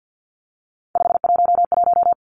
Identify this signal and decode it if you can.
Morse code, heard as a keyed tone, about 50 wpm, 700 Hz. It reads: 511
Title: Three Challenging Letter-Number Combinations 50wpm